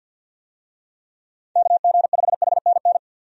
Morse code, heard as a keyed tone, about 46 wpm, 700 Hz. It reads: KG5HNN
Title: USA Call Signs - 6 Characters - 50wpm